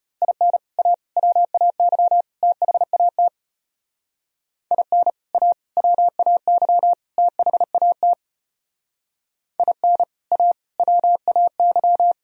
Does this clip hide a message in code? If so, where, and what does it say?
Yes: Morse code from 0.2 to 12.1 s, heard as a keyed tone, about 34 wpm, 700 Hz. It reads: IN A WAY THAT IN A WAY THAT IN A WAY